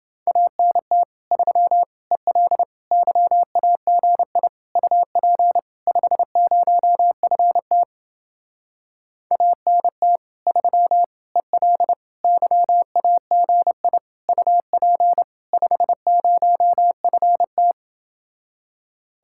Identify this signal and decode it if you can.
Morse code, heard as a keyed tone, about 28 wpm, 700 Hz. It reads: ANT 3 EL YAGI UP 50FT ANT 3 EL YAGI UP 50FT